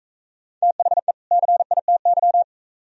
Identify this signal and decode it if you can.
Morse code, heard as a keyed tone, about 38 wpm, 700 Hz. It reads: THE CITY